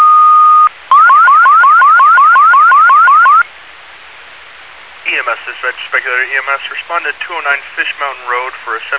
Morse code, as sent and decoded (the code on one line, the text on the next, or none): none